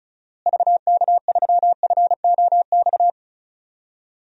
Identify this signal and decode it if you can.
Morse code, heard as a keyed tone, about 34 wpm, 700 Hz. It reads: VK3FOX
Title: DXCC-100 Call Signs - 6 Characters - Repeated 3x - 35wpm